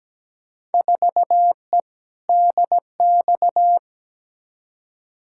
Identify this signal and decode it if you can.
Morse code, heard as a keyed tone, about 17 wpm, 700 Hz. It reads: 4E DX
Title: Field Day Exchanges Encore - 17wpm